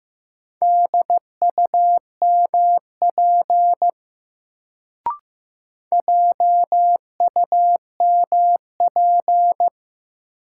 Morse code, heard as a keyed tone, about 15 wpm, 700 Hz. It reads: DUMP JUMP
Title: Sentences from Top 1300 Words - New Word Drill Alt - 15wpm